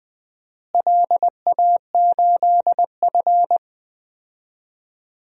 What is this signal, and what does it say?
Morse code, heard as a keyed tone, about 21 wpm, 700 Hz. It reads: LA8F
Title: DXCC-100 Call Signs - 4 Characters - Repeated 3x - 20wpm